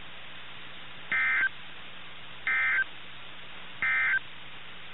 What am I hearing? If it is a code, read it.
Emergency Alert System (SAME) tones, heard as harsh buzzing bursts. It is heard from 1.0 to 4.3 s.
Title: NOAA Weather Radio - KIH61